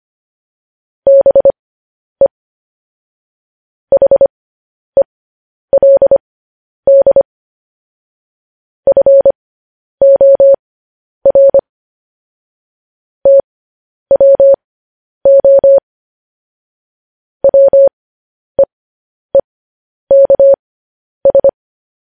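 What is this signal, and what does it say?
Morse code, heard as a keyed tone, about 25 wpm, spaced as for 10 wpm, 550 Hz. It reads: BE HELD FOR TWO WEEKS
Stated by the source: Morse code news headlines practice transmission.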